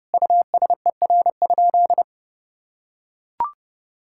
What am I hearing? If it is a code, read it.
Morse code, heard as a keyed tone, about 32 wpm, 700 Hz. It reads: USER?
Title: Sentences from Top 1000 Words - Review Part 5 - 30wpm